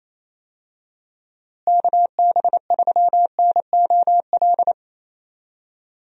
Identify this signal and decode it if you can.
Morse code, heard as a keyed tone, about 28 wpm, 700 Hz. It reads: KB3NOL
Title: USA Call Signs - 6 Characters - Repeated 2x - 28wpm